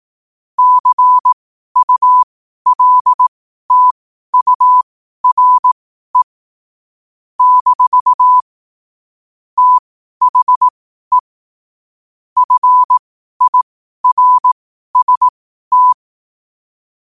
Morse code, heard as a keyed tone, about 18 wpm, spaced as for 12 wpm, 1000 Hz. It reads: CULTURE - THE FIRST
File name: qotdfiles / 12wpm